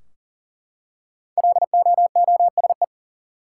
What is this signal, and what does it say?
Morse code, heard as a keyed tone, about 40 wpm, 700 Hz. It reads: LOOSE
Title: Five Letter Words - 40wpm